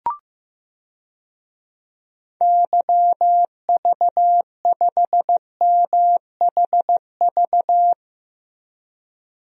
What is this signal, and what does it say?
Morse code, heard as a keyed tone, about 15 wpm, 700 Hz. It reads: YV5MHV
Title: DXCC-100 Call Signs - 6 Characters - Repeated 3x - 15wpm